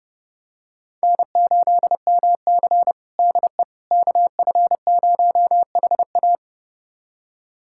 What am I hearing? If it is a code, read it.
Morse code, heard as a keyed tone, about 31 wpm, 700 Hz. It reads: N8MC DE KF0HA
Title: USA Callsign DE Callsign 30wpm